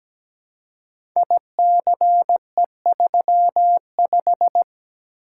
Morse code, heard as a keyed tone, about 17 wpm, 700 Hz. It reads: ICE35
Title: Code Groups - 5 Letter:Numbers - 17wpm